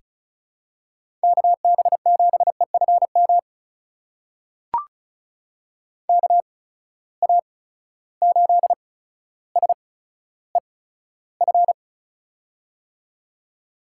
Morse code, heard as a keyed tone, about 36 wpm, spaced as for 15 wpm, 700 Hz. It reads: KB7EFM KA8SEF
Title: USA Call Signs - 6 Characters - 8x Character Spacing - 35wpm